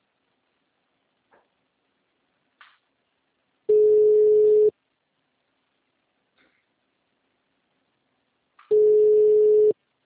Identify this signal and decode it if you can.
Ringback tone (Europe, 425 Hz).
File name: out- -2006- - - .102